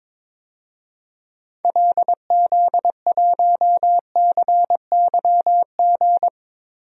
Morse code, heard as a keyed tone, about 22 wpm, 700 Hz. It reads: LZ1CYG
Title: DXCC-100 Call Signs - 6 Characters - Repeated 2x - 22wpm